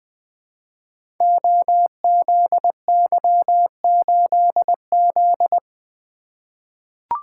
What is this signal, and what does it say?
Morse code, heard as a keyed tone, about 21 wpm, 700 Hz. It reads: OZY8Z
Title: Code Groups - 5 Letter:Numbers - 20wpm